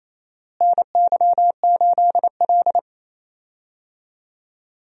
Morse code, heard as a keyed tone, about 28 wpm, 700 Hz. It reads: NY8L
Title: USA Call Signs - 4 Characters - Repeated 2x - 28wpm